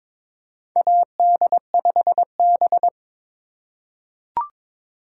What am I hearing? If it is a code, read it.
Morse code, heard as a keyed tone, about 22 wpm, 700 Hz. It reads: AD5B
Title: USA Call Signs - 4 Characters - 3x Character Spacing - 22wpm